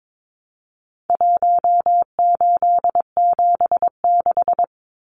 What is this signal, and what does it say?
Morse code, heard as a keyed tone, about 22 wpm, 700 Hz. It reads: 1876
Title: Numbers - Contest Style - 4-Digit 22wpm